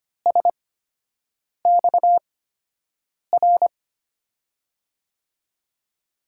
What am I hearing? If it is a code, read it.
Morse code, heard as a keyed tone, about 25 wpm, spaced as for 7 wpm, 700 Hz. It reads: SXR
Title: USA Call Signs - 5 Characters - 8x Character Spacing - 25wpm